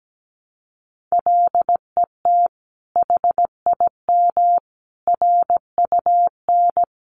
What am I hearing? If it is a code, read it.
Morse code, heard as a keyed tone, about 17 wpm, 700 Hz. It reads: LET HIM RUN
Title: Sentences from Top 1300 Words 17wpm